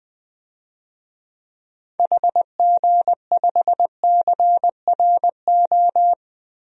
Morse code, heard as a keyed tone, about 21 wpm, 700 Hz. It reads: HG5CRO
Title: DXCC-100 Call Signs - 6 Characters - Repeated 3x - 20wpm